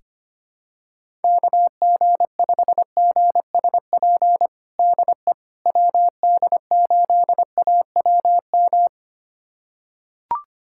Morse code, heard as a keyed tone, about 25 wpm, 700 Hz. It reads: KG5GSP DE WD8AWM
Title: USA Callsign DE Callsign 25wpm